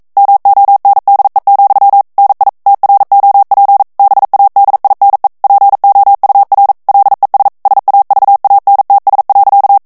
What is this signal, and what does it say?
Morse code, heard as a keyed tone, about 42 wpm, 800 Hz. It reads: MONDE, NI TROP BADINE POUR LES SAVANTS.
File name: CW / FICHIERS-MP3 / FRANCAIS TEXTE CLAIR 2 VITESSE 40 MOTS MINUTES